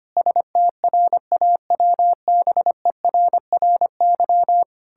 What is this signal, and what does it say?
Morse code, heard as a keyed tone, about 25 wpm, 700 Hz. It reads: STRAWBERRY
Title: Pangram Challenge v2 - 25wpm